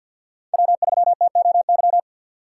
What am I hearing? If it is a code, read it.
Morse code, heard as a keyed tone, about 51 wpm, 700 Hz. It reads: W3TOY
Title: USA Call Signs - 5 Characters - 2x Character Spacing - 50wpm